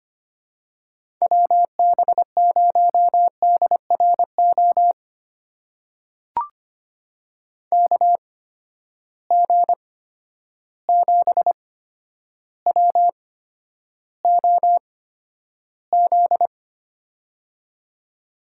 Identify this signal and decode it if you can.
Morse code, heard as a keyed tone, about 25 wpm, spaced as for 11 wpm, 700 Hz. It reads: WB0DRO KG7WOZ
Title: USA Call Signs - 6 Characters - 8x Character Spacing - 25wpm